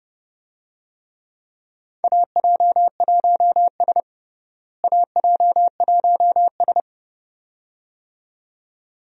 Morse code, heard as a keyed tone, about 30 wpm, 700 Hz. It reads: AJ1S AJ1S
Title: USA Call Signs - 4 Characters - Repeated 2x - 30wpm